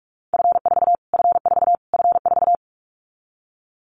Morse code, heard as a keyed tone, about 44 wpm, 700 Hz. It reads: F4 F4 F4
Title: Callsign Prefixes for Top 25 DXCC - Repeated 3x - 45wpm